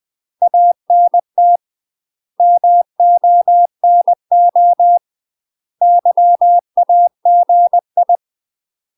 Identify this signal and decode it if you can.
Morse code, heard as a keyed tone, about 20 wpm, 700 Hz. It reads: ANT MONO YAGI
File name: UK MASTER QSOs TRAINING/20WPM